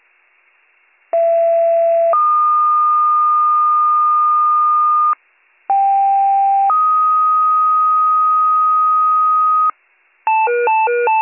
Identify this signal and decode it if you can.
Two-tone page, heard as a one-second tone, then a three-second tone, about 670 then 1220 Hz, about 790 then 1250 Hz. It is heard from 1.1 to 9.7 s.